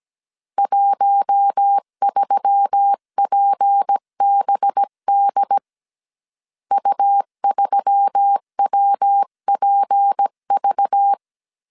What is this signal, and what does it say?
Morse code, heard as a keyed tone, about 17 wpm, 800 Hz. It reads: 13PBD U3WPV
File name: CW / DET / 1 6 1500C S C 16wpm 2733